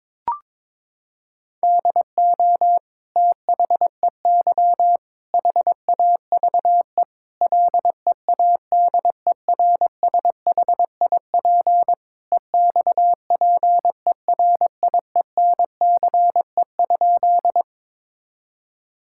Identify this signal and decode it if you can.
Morse code, heard as a keyed tone, about 21 wpm, 700 Hz. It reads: DO THEY HAVE LEADERSHIP EXPERIENCE?
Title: Sentences from Top 1600 Words 22wpm